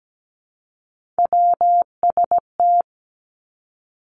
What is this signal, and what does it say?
Morse code, heard as a keyed tone, about 17 wpm, 700 Hz. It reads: WST
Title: Three Challenging Letter-Number Combinations 17wpm